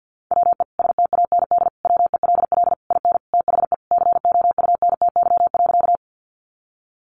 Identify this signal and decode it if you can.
Morse code, heard as a keyed tone, about 50 wpm, 700 Hz. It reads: WE STAND WELL IN THE COUNTY.
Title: Sentences from Top 1200 Words 50wpm